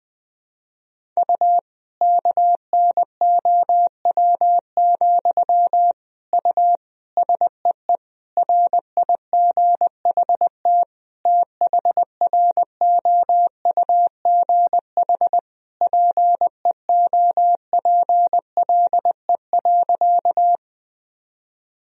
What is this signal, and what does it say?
Morse code, heard as a keyed tone, about 20 wpm, 700 Hz. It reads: U KNOW, U SEE RIGHT THROUGH PEOPLE.
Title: Sentences from Top 200 Words and Abbreviations 20wpm